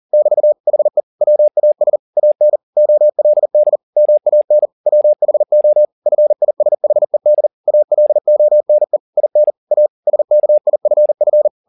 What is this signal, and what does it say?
Morse code, heard as a keyed tone, about 40 wpm, 600 Hz. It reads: <BT> HE WAS AN OLD MAN WHO FISHED ALONE IN A SKIFF
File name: the old man and the sea 1 40wpm